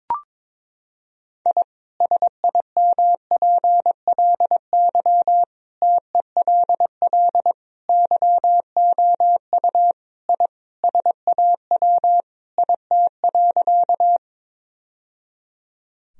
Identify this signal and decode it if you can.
Morse code, heard as a keyed tone, about 22 wpm, 700 Hz. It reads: I SIMPLY TELL YOU I SAW IT.
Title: Sentences from Top 1000 Words - Review Part 4 - 22wpm